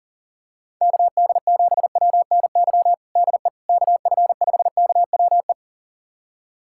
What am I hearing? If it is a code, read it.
Morse code, heard as a keyed tone, about 40 wpm, 700 Hz. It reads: KD7WNY DE KF5KWE
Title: USA Callsign DE Callsign - 6 Characters - 40wpm